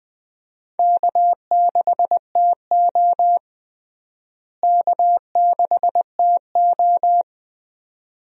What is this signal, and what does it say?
Morse code, heard as a keyed tone, about 21 wpm, 700 Hz. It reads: K6TO K6TO
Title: USA Call Signs - 4 Characters - Repeated 2x - 20wpm